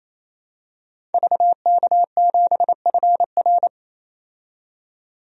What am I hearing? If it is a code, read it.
Morse code, heard as a keyed tone, about 28 wpm, 700 Hz. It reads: VK7FR
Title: DXCC-100 Call Signs - 5 Characters - 28wpm